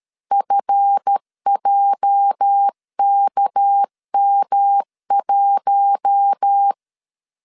Morse code, heard as a keyed tone, about 13 wpm, 800 Hz. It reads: FJKM1
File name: LEZIONI / 8 YJ45 / 8 YJ 12wpm 2235